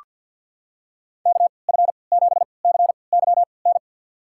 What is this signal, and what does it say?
Morse code, heard as a keyed tone, about 46 wpm, spaced as for 27 wpm, 700 Hz. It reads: KF7CYN
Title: USA Call Signs - 6 Characters - 3x Character Spacing - 50wpm